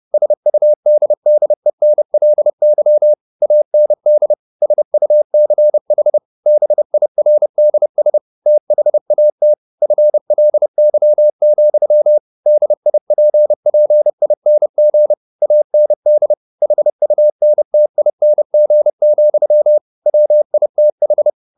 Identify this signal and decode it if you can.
Morse code, heard as a keyed tone, about 30 wpm, 600 Hz. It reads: SUDDENLY AND SUCH BIRDS THAT FLY, DIPPING AND HUNTING, WITH
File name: the old man and the sea / the old man and the sea 1 30wpm